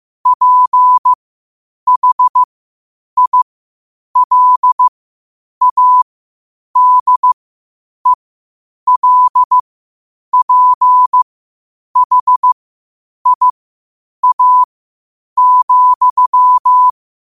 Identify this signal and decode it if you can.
Morse code, heard as a keyed tone, about 15 wpm, spaced as for 8 wpm, 1000 Hz. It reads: PHILADELPHIA,
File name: qotdfiles / 8wpm